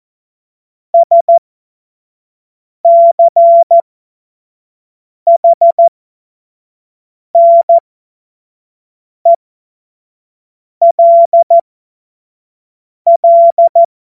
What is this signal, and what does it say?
Morse code, heard as a keyed tone, about 14 wpm, spaced as for 5 wpm, 680 Hz. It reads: SCHNELL
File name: Morsekurs / Witze / Witz 16 April 2024 25 F70